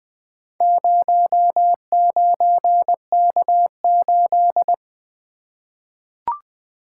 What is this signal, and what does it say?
Morse code, heard as a keyed tone, about 20 wpm, 700 Hz. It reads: 09K8
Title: Four Letter-Number Combinations - 20wpm